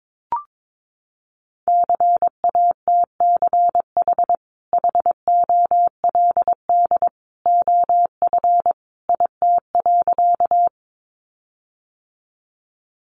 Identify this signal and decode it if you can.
Morse code, heard as a keyed tone, about 22 wpm, 700 Hz. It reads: CATCH HOLD OF IT.
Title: Sentences from Top 1500 Words 22wpm